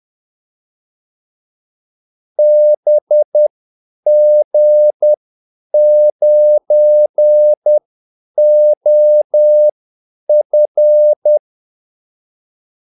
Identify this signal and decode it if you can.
Morse code, heard as a keyed tone, about 10 wpm, spaced as for 8 wpm, 600 Hz. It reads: BG9OF
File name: Callsigns-10-8wpm